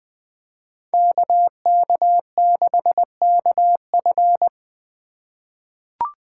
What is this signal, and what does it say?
Morse code, heard as a keyed tone, about 20 wpm, 700 Hz. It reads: KK6KF
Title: USA Call Signs - 5 Characters - 4x Character Spacing - 20wpm